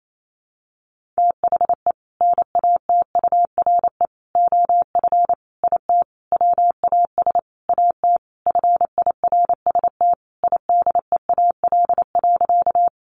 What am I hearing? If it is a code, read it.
Morse code, heard as a keyed tone, about 28 wpm, 700 Hz. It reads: THE NATURE OF IT WAS AT FIRST IDEAL.
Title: Sentences from Top 1400 Words 28wpm